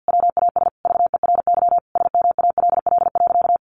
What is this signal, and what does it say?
Morse code, heard as a keyed tone, about 50 wpm, 700 Hz. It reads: WAS VERY SMALL.
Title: Sentences from Top 400 Words 50wpm